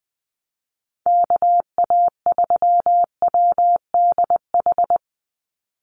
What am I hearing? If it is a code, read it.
Morse code, heard as a keyed tone, about 20 wpm, 700 Hz. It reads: KA3WDH
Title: USA Call Signs 20wpm